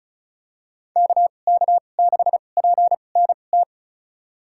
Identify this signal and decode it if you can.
Morse code, heard as a keyed tone, about 36 wpm, spaced as for 25 wpm, 700 Hz. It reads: KK6PNT
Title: USA Call Signs - 2x Character Spacing - 35wpm